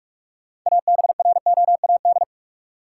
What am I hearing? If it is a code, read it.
Morse code, heard as a keyed tone, about 46 wpm, 700 Hz. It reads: ABROAD
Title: Six Letter Words - 45wpm